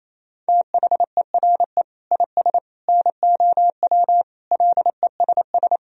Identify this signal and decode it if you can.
Morse code, heard as a keyed tone, about 28 wpm, 700 Hz. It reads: THERE IS NOW LESS
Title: Sentences from Top 1000 Words - Review Part 4 - 28wpm